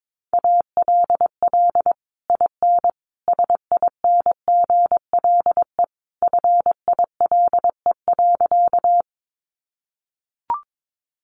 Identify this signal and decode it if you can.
Morse code, heard as a keyed tone, about 22 wpm, 700 Hz. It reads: ALL IN SINGLE FILE.
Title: Sentences from Top 1200 Words 22wpm